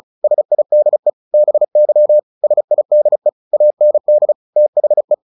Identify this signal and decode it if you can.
Morse code, heard as a keyed tone, about 34 wpm, 600 Hz. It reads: SIDE BY SIDE AND THE
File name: the old man and the sea 2 35wpm